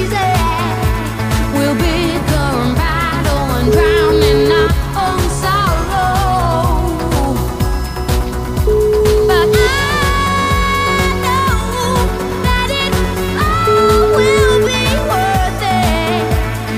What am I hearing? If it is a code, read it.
Ringback tone (Europe, 425 Hz), with noise.